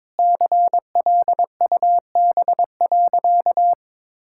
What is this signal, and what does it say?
Morse code, heard as a keyed tone, about 22 wpm, 700 Hz. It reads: CLUB.
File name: practice / 22wpm / Sentences with 3-Word Phrases - 22wpm